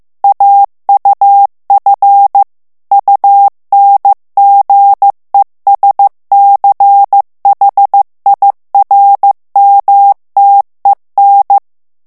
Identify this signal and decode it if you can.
Morse code, heard as a keyed tone, about 15 wpm, 800 Hz. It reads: AUF UNGESCHIRMTEN
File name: morse mp3 / ALLEMAND texte clair 1 vitesse 14 mots minute